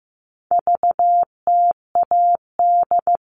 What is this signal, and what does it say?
Morse code, heard as a keyed tone, about 15 wpm, 700 Hz. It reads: VTAD
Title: All Cut Numbers - 4-Digit 15wpm